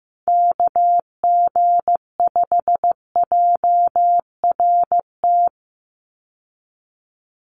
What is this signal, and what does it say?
Morse code, heard as a keyed tone, about 15 wpm, 700 Hz. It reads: KG5JRT
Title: USA Call Signs 15wpm